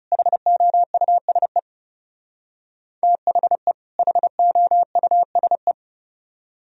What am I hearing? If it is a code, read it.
Morse code, heard as a keyed tone, about 37 wpm, 700 Hz. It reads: HOUSE THE HOUSE
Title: Speed-Racing - Sets of 2 Words 30wpm